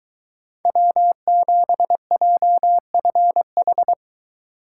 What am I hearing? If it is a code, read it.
Morse code, heard as a keyed tone, about 23 wpm, 700 Hz. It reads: W7JFH